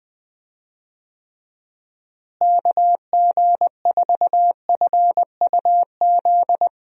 Morse code, heard as a keyed tone, about 20 wpm, 700 Hz. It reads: KG4FUZ
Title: USA Call Signs - 6 Characters - Repeated 2x - 20wpm